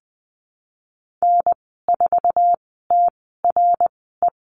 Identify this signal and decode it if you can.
Morse code, heard as a keyed tone, about 20 wpm, spaced as for 15 wpm, 700 Hz. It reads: N4TRE
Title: 055 Lesson 10g - Letter r - Call Signs 20wpm